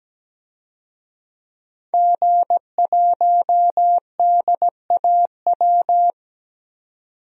Morse code, heard as a keyed tone, about 17 wpm, 700 Hz. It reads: G1DAW
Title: DXCC-100 Call Signs - 5 Characters - 17wpm